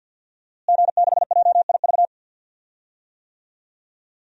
Morse code, heard as a keyed tone, about 52 wpm, 700 Hz. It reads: G6JIV
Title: DXCC-100 Call Signs - 5 Characters - Repeated 3x - 50wpm